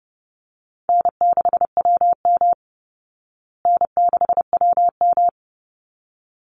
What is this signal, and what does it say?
Morse code, heard as a keyed tone, about 30 wpm, 700 Hz. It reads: N6WM N6WM
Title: Call Signs Repeated 3x - 30wpm